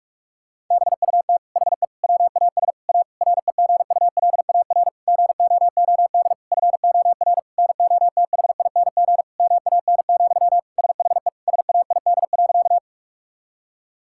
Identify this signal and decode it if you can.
Morse code, heard as a keyed tone, about 46 wpm, 700 Hz. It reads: BUT HE WAS A REGULAR GOOD FOR NOTHING MAN, SHE SAID.
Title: Sentences from Top 1000 Words - Review Part 6 - 45wpm